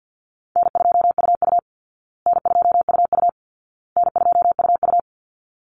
Morse code, heard as a keyed tone, about 50 wpm, 700 Hz. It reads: N2UU N2UU N2UU
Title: Easy Call Signs Repeated 3x - 50wpm